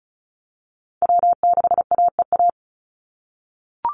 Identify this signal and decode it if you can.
Morse code, heard as a keyed tone, about 34 wpm, 700 Hz. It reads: W6AEA
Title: Call Signs - 35wpm